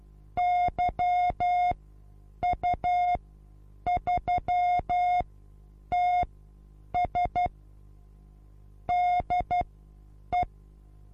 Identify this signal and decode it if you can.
Morse code, heard as a keyed tone, about 12 wpm, spaced as for 8 wpm, 720 Hz. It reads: YU3TS DE
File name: Onlinelehrgang / morsen / DJ4UF Morsen23